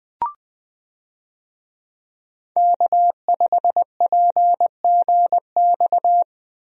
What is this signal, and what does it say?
Morse code, heard as a keyed tone, about 20 wpm, 700 Hz. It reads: K5PGX
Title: USA Call Signs - 5 Characters - Repeated 2x - 20wpm